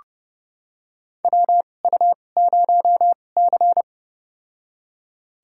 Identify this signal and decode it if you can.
Morse code, heard as a keyed tone, about 30 wpm, spaced as for 22 wpm, 690 Hz. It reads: WU0C
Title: USA Call Signs - 4 Characters - 2x Character Spacing - 30wpm